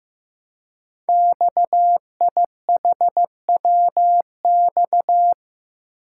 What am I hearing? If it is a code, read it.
Morse code, heard as a keyed tone, about 15 wpm, 700 Hz. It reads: XIHWX
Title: Code Groups - 5 Letters - 15wpm